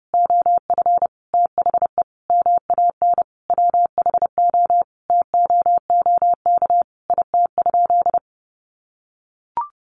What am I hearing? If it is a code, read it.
Morse code, heard as a keyed tone, about 30 wpm, 700 Hz. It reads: OF THE MAN WHO TOOK IT?
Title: Sentences from Top 300 Words 30wpm